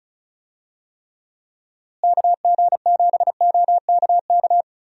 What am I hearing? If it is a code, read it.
Morse code, heard as a keyed tone, about 34 wpm, 700 Hz. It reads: KG7OKK
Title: USA Call Signs - 6 Characters - Repeated 2x - 35wpm